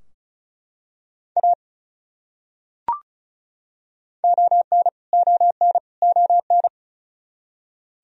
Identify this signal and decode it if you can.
Morse code, heard as a keyed tone, about 36 wpm, 700 Hz. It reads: A ON ON ON
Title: Top 100 Words Repeated 35wpm